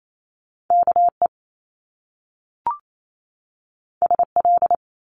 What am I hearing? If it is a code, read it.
Morse code, heard as a keyed tone, about 28 wpm, 700 Hz. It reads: K E S L
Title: Top Two Letter Combinations in Words 28wpm